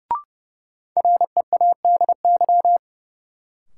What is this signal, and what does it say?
Morse code, heard as a keyed tone, about 32 wpm, 700 Hz. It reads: READY
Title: Sentences from Top 400 Words - New Word Drill - 30wpm